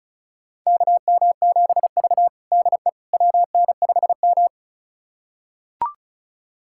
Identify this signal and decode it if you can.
Morse code, heard as a keyed tone, about 36 wpm, 700 Hz. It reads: KM7V DE WN5M
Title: USA Callsign DE Callsign - 4 Characters - 35wpm